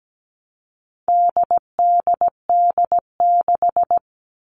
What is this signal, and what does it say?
Morse code, heard as a keyed tone, about 17 wpm, 700 Hz. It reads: DDD6
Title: All Cut Numbers - 4-Digit 17wpm